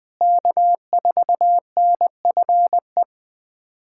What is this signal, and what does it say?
Morse code, heard as a keyed tone, about 20 wpm, 700 Hz. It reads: K4NFE
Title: USA Callsign DE Callsign 20wpm